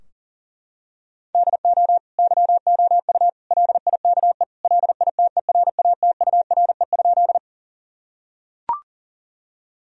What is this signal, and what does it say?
Morse code, heard as a keyed tone, about 40 wpm, 700 Hz. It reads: DO YOU LIKE LITERATURE?
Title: Sentences from Top 1000 Words - Review Part 3 - 40wpm